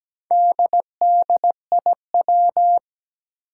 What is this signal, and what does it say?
Morse code, heard as a keyed tone, about 17 wpm, 700 Hz. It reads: DDIW
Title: Four Challenging Letter-Number Combinations 17wpm